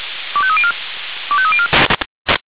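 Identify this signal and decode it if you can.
Five-tone selective call (ZVEI), heard as quick runs of five tones, tones of about 70 ms. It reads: 25304 25304
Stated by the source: ZVEI five-tone selective call received on SDR.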